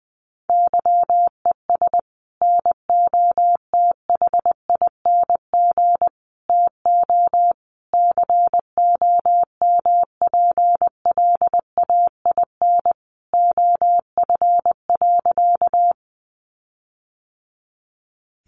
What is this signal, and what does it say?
Morse code, heard as a keyed tone, about 20 wpm, 700 Hz. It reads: YES NOTHING TO COMPLAIN OF.
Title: Sentences from Top 1300 Words 20wpm